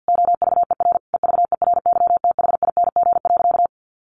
Morse code, heard as a keyed tone, about 50 wpm, 700 Hz. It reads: OVER EVERYTHING.
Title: Sentences from Top 900 Words 50wpm